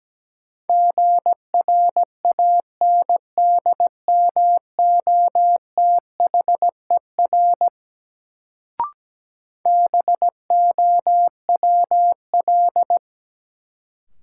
Morse code, heard as a keyed tone, about 17 wpm, 700 Hz. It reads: GRANDMOTHER BOWL
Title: Sentences from Top 1200 Words - New Word Drill Alt - 17wpm